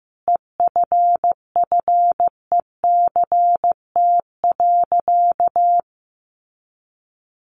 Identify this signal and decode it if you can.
Morse code, heard as a keyed tone, about 15 wpm, 700 Hz. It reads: EFFECT.
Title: Sentences from Top 1000 Words 15wpm